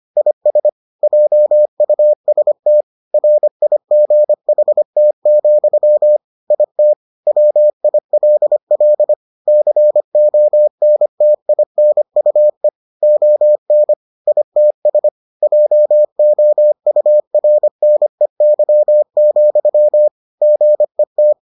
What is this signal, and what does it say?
Morse code, heard as a keyed tone, about 25 wpm, 600 Hz. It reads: IS JUST RIGHT, IT WILL CONTINUE ON ITS JOURNEY, GET